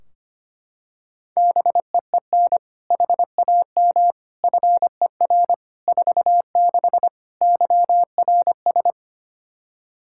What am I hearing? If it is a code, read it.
Morse code, heard as a keyed tone, about 25 wpm, 700 Hz. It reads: BEEN HAM FER 46 YRS
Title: K1USN SST QSO Practice 25wpm